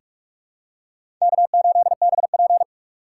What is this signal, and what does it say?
Morse code, heard as a keyed tone, about 46 wpm, 700 Hz. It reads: K8BP
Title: USA Call Signs - 2x Character Spacing - 45wpm